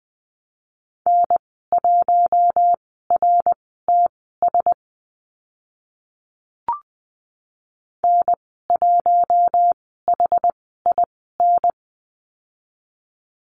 Morse code, heard as a keyed tone, about 20 wpm, spaced as for 15 wpm, 700 Hz. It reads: N1RTS N1HIN
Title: 069 Lesson 12g - Letter d - Call Signs 20wpm